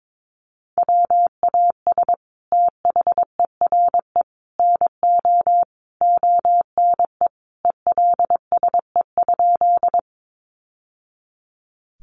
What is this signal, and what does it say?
Morse code, heard as a keyed tone, about 22 wpm, 700 Hz. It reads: WAS THERE NO ONE ELSE?
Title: Sentences from Top 800 Words 22wpm